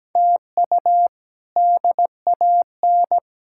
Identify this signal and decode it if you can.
Morse code, heard as a keyed tone, about 17 wpm, 700 Hz. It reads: TU DAN
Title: K1USN SST QSO Practice - 17wpm